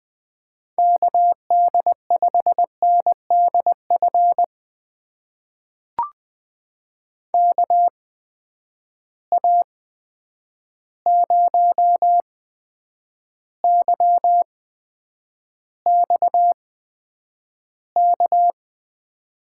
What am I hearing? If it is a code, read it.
Morse code, heard as a keyed tone, about 20 wpm, spaced as for 9 wpm, 700 Hz. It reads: KD5NDF KA0YXK
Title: USA Call Signs - 6 Characters - 8x Character Spacing - 20wpm